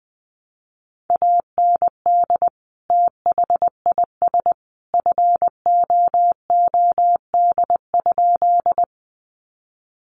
Morse code, heard as a keyed tone, about 20 wpm, 700 Hz. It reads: AND THIS FOOD?
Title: Sentences from Top 300 Words 20wpm